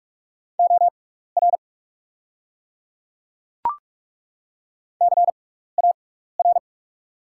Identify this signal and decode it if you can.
Morse code, heard as a keyed tone, about 42 wpm, spaced as for 15 wpm, 700 Hz. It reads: OR CAR
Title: 6x Character Spacing - Top 300 Words - 45wpm